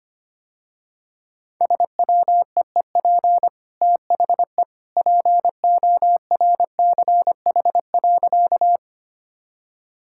Morse code, heard as a keyed tone, about 25 wpm, 700 Hz. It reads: SWEEP THE PORCH.
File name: Easy Sentences with Hints - 25wpm